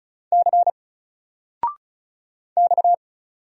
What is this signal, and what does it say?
Morse code, heard as a keyed tone, about 35 wpm, 700 Hz. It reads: C X
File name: Single-Letter-Number-35wpm